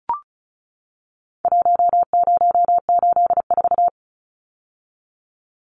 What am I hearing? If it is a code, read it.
Morse code, heard as a keyed tone, about 34 wpm, 700 Hz. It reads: 1084
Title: Numbers - Contest Style - 4-Digit 35wpm